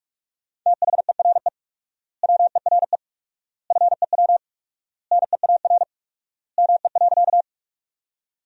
Morse code, heard as a keyed tone, about 46 wpm, 700 Hz. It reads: THERE WERE FEW NEAR ME.
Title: 4x Word-Spacing - Sentences from Top 300 Words - 45wpm